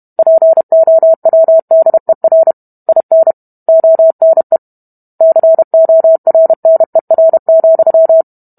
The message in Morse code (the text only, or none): POWDER IN ONE CORNER,